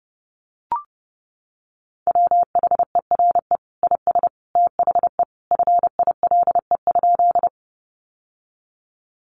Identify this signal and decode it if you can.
Morse code, heard as a keyed tone, about 30 wpm, 700 Hz. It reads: WHERE IS THE FILE?
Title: Sentences from Top 1200 Words 30wpm